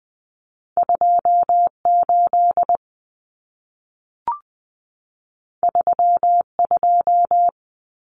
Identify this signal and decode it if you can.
Morse code, heard as a keyed tone, about 20 wpm, 700 Hz. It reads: 28 32
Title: Common Cut Numbers - 2-Digit 20wpm